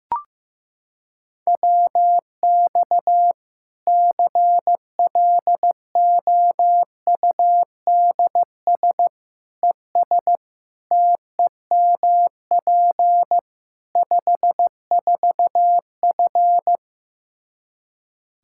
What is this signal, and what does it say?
Morse code, heard as a keyed tone, about 15 wpm, 700 Hz. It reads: WX CLOUDS ES TEMP 54F
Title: Practice QSO - Exchange 2 - Info - 15wpm